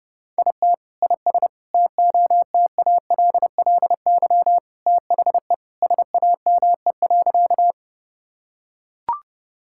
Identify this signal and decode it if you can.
Morse code, heard as a keyed tone, about 30 wpm, 700 Hz. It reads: IT IS TOTALLY THE SAME.
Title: Sentences from Top 1600 Words 30wpm